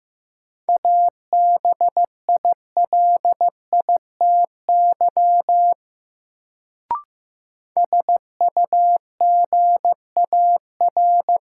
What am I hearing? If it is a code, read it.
Morse code, heard as a keyed tone, about 15 wpm, 700 Hz. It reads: ABILITY SUGAR
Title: Sentences from Top 1200 Words - New Word Drill Alt - 15wpm